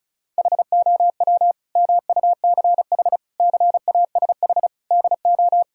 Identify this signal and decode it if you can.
Morse code, heard as a keyed tone, about 34 wpm, 700 Hz. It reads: HOW MUCH CASH DO
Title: Sentences from Top 1000 Words - Review Part 5 - 35wpm